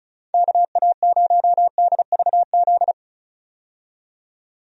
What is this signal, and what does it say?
Morse code, heard as a keyed tone, about 36 wpm, 700 Hz. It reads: KA0DVZ
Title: USA Call Signs - 6 Characters - 35wpm